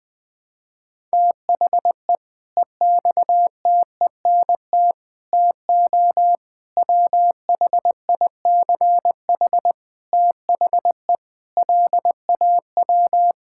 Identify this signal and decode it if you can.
Morse code, heard as a keyed tone, about 20 wpm, 700 Hz. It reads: THE EXTENT TO WHICH THE LAW
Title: Sentences with 5-Word Phrases - 20wpm